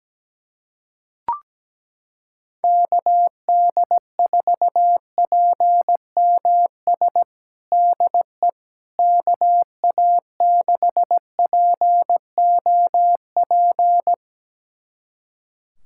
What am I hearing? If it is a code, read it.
Morse code, heard as a keyed tone, about 17 wpm, 700 Hz. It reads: KD4PMS DE KA6POP
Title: USA Callsign DE Callsign 17wpm